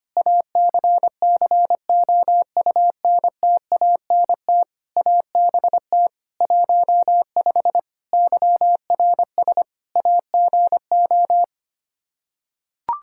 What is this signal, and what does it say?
Morse code, heard as a keyed tone, about 25 wpm, 700 Hz. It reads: ACCOUNTANT ABT 15 YRS AGO
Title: Practice QSO - Exchange 3 - More Info - 25wpm